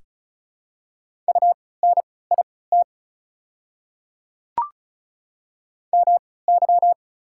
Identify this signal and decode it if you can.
Morse code, heard as a keyed tone, about 34 wpm, spaced as for 20 wpm, 690 Hz. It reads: UNIT MY
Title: 3x Character Spacing - Top 500 Words - 35wpm